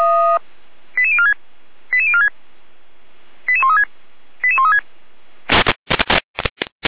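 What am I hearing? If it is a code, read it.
Five-tone selective call (ZVEI), heard as quick runs of five tones, tones of about 70 ms. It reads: 88046 88046 88136 88136
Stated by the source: ZVEI five-tone selective call received with an SDR.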